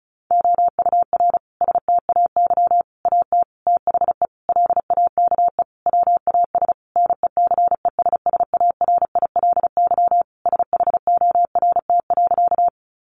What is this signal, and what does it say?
Morse code, heard as a keyed tone, about 34 wpm, 700 Hz. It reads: OUR STAY AT THE LAKE WAS NECESSARILY SHORT.
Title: Sentences from Top 1400 Words 35wpm